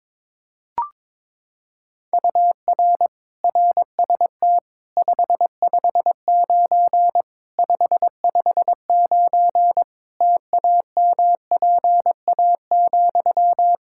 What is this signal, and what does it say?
Morse code, heard as a keyed tone, about 22 wpm, 700 Hz. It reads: UR RST 559 559 TAMPA,
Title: Practice QSO - Exchange 1 - Beginning - 22wpm